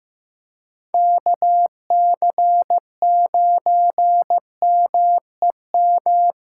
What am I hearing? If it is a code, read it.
Morse code, heard as a keyed tone, about 15 wpm, 700 Hz. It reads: KC9MEM
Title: USA Callsign DE Callsign - 6 Characters - 15wpm